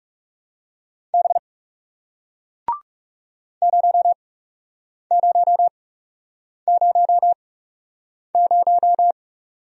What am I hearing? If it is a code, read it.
Morse code, heard as a keyed tone, about 39 wpm, spaced as for 8 wpm, 700 Hz. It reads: B 0000